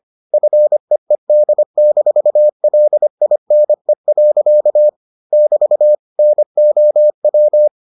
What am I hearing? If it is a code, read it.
Morse code, heard as a keyed tone, about 25 wpm, 600 Hz. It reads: FEED-LINE. <BT> NOW